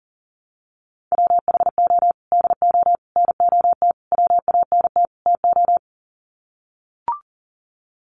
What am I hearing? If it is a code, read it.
Morse code, heard as a keyed tone, about 40 wpm, 700 Hz. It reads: WHO DO NOT WANT TO
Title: Sets of 5 Words Encore2 40wpm